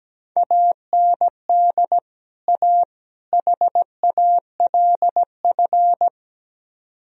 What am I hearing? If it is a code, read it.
Morse code, heard as a keyed tone, about 17 wpm, 700 Hz. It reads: AND A HALF
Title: Speed-Racing - Sets of 3 Words 17wpm